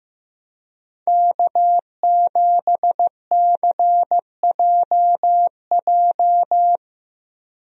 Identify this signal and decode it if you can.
Morse code, heard as a keyed tone, about 15 wpm, 700 Hz. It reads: K7CJJ
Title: USA Call Signs - 5 Characters - Repeated 3x - 15wpm